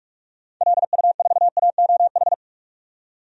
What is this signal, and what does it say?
Morse code, heard as a keyed tone, about 48 wpm, 700 Hz. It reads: LU4AOH
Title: DXCC-100 Call Signs - 6 Characters - 45wpm